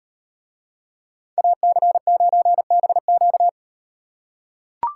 Morse code, heard as a keyed tone, about 38 wpm, 700 Hz. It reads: AC9BQ